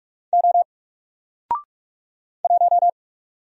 Morse code, heard as a keyed tone, about 44 wpm, 700 Hz. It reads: O 1
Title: Single Letter-Number Alt 45wpm